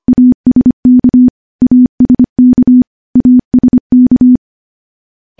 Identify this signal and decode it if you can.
Morse code, heard as a keyed tone, about 25 wpm, 250 Hz. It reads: ASK ASK ASK
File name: morse / morse-training-25WPM